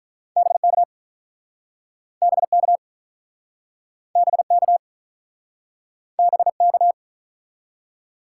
Morse code, heard as a keyed tone, about 48 wpm, 700 Hz. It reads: BK BK BK BK